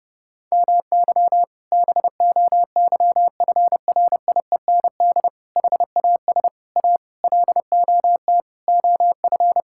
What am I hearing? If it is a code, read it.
Morse code, heard as a keyed tone, about 30 wpm, 700 Hz. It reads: MY BOYFRIEND HAS A LOT OF
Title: Sentences from Top 1000 Words - Review Part 5 - 30wpm